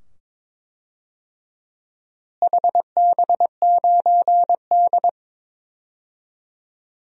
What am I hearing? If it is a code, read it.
Morse code, heard as a keyed tone, about 22 wpm, 700 Hz. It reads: HB9D
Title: DXCC-100 Call Signs - 4 Characters - Repeated 2x - 22wpm